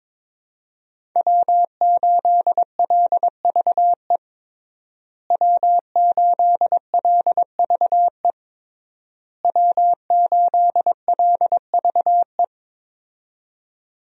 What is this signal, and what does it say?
Morse code, heard as a keyed tone, about 21 wpm, 700 Hz. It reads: W8LVE W8LVE W8LVE
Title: USA Call Signs - 5 Characters - Repeated 3x - 22wpm